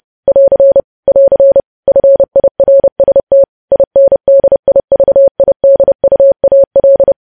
Morse code, heard as a keyed tone, about 30 wpm, 550 Hz. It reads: <AR> <AR> FIRST INDIVIDUAL
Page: news headlines in morse code at 30 wpm